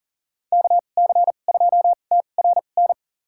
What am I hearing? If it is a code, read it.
Morse code, heard as a keyed tone, about 40 wpm, spaced as for 29 wpm, 700 Hz. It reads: KC2TRN
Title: USA Call Signs - 6 Characters - 2x Character Spacing - 40wpm